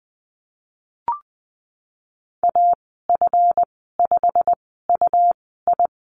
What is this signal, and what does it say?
Morse code, heard as a keyed tone, about 20 wpm, spaced as for 15 wpm, 700 Hz. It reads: AF5UI
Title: 161 Lesson 26g - Letter g - Call Signs 20wpm